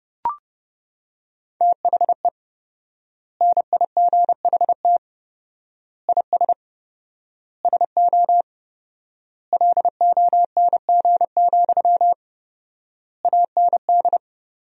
Morse code, heard as a keyed tone, about 31 wpm, 700 Hz. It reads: THE NIGHT IS SO LONG, AND
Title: 4x Word-Spacing - Sentences from Top 500 Words - 30wpm